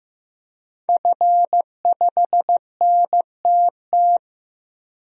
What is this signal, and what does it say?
Morse code, heard as a keyed tone, about 15 wpm, 700 Hz. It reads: F5NTT
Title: DXCC-100 Call Signs - 5 Characters - Repeated 2x - 15wpm